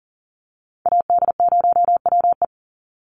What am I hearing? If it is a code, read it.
Morse code, heard as a keyed tone, about 40 wpm, 700 Hz. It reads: AD0WE
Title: Call Signs - 40wpm